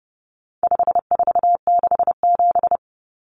Morse code, heard as a keyed tone, about 30 wpm, 700 Hz. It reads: 5467
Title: Common Cut Numbers - 4-Digit 30wpm